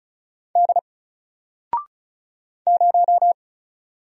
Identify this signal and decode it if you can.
Morse code, heard as a keyed tone, about 35 wpm, 700 Hz. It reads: D 0
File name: Single-Letter-Number-35wpm